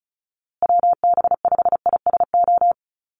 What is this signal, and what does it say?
Morse code, heard as a keyed tone, about 34 wpm, 700 Hz. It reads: WB5ISO
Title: USA Call Signs Repeated 3x 35wpm